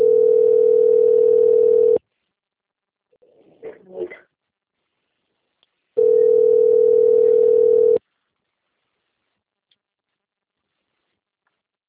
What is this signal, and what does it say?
Ringback tone (North America), 440 + 480 Hz.
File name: out- - - - - .14483